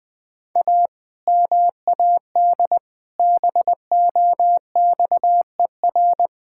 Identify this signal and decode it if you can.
Morse code, heard as a keyed tone, about 20 wpm, 700 Hz. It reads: A MAD BOXER
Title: Pangram Challenge v2 - 20wpm